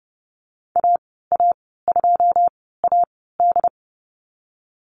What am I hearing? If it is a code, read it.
Morse code, heard as a keyed tone, about 30 wpm, spaced as for 17 wpm, 700 Hz. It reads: AA2AD
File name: Lesson 14g - Number 2 - Call Signs 30wpm